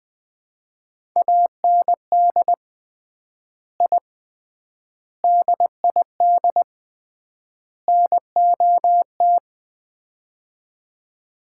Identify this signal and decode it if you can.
Morse code, heard as a keyed tone, about 20 wpm, 700 Hz. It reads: AND I DID NOT
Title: 3x Word-Spacing - Sets of 4 Words - 20wpm